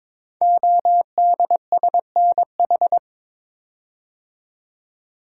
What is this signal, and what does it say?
Morse code, heard as a keyed tone, about 22 wpm, 700 Hz. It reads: ODSNH
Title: Code Groups - 5 Letter:Numbers - 22wpm